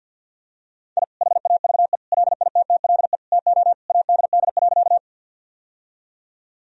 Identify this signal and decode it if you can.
Morse code, heard as a keyed tone, about 50 wpm, 700 Hz. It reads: I HAVE LITTLE TO ADD.
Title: Sentences from Top 1000 Words - Review Part 1 - 50wpm